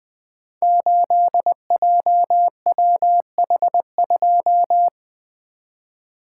Morse code, heard as a keyed tone, about 20 wpm, 700 Hz. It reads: 8JWH2
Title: Code Groups - 5 Letter:Numbers - 20wpm